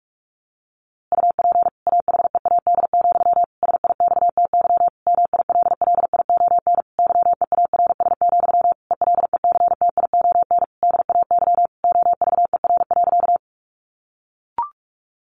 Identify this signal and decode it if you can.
Morse code, heard as a keyed tone, about 44 wpm, 700 Hz. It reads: UP AHEAD, SIXTY MILLION YEARS, ELECTION DAY OVER.
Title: A Sound Of Thunder 45wpm